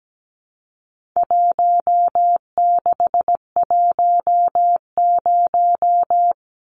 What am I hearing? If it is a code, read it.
Morse code, heard as a keyed tone, about 17 wpm, 700 Hz. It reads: 1610
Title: Numbers - 4-Digit 17wpm